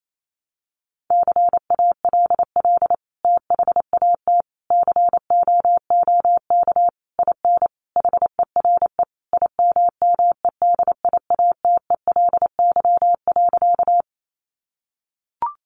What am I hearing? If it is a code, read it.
Morse code, heard as a keyed tone, about 28 wpm, 700 Hz. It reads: CALL THAT COOK IN HERE IMMEDIATELY.
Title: Sentences from Top 1100 Words 28wpm